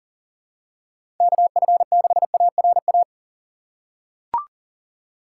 Morse code, heard as a keyed tone, about 40 wpm, 700 Hz. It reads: KF6ARA
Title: USA Call Signs - 6 Characters - 6x Character Spacing - 40wpm